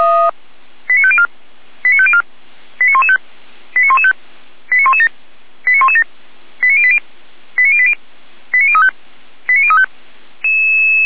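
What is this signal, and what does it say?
Five-tone selective call (ZVEI), heard as quick runs of five tones, tones of about 70 ms. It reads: 89504 89504 89115 89115 89117 89117 89080 89080 89035 89035
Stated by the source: ZVEI five-tone selective call, received on an SDR.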